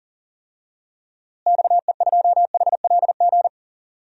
Morse code, heard as a keyed tone, about 40 wpm, 700 Hz. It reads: XE2HLG
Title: DXCC-100 Call Signs - 6 Characters - Repeated 2x - 40wpm